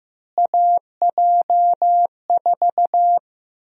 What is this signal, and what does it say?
Morse code, heard as a keyed tone, about 15 wpm, 700 Hz. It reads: AJ4
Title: USA Call Sign Prefixes - Repeated 2x - 15wpm